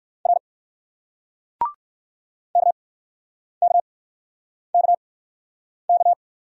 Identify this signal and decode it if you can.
Morse code, heard as a keyed tone, about 58 wpm, spaced as for 9 wpm, 700 Hz. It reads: R KKKK